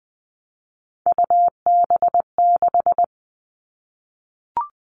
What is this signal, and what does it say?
Morse code, heard as a keyed tone, about 20 wpm, 700 Hz. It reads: UB6
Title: All Cut Numbers - 3-Digit 20wpm